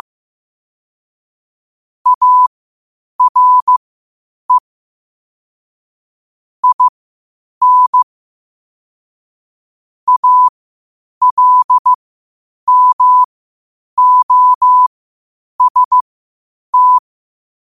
Morse code, heard as a keyed tone, about 15 wpm, spaced as for 8 wpm, 1000 Hz. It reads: ARE IN ALMOST
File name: qotdfiles / 8wpm